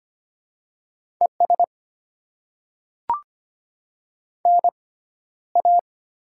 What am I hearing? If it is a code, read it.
Morse code, heard as a keyed tone, about 25 wpm, spaced as for 13 wpm, 700 Hz. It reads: ES NA
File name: Top 20 QSO Elements - 6x Character Spacing - 25wpm